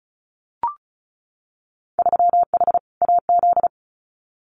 Morse code, heard as a keyed tone, about 34 wpm, 700 Hz. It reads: 3H AZ
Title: Winter Field Day Exchanges Encore 35wpm